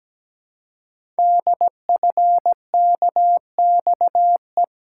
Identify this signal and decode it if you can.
Morse code, heard as a keyed tone, about 17 wpm, 700 Hz. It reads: DFKXE
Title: Code Groups - 5 Letters - 17wpm